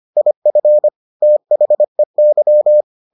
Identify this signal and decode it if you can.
Morse code, heard as a keyed tone, about 25 wpm, 600 Hz. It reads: IF THEY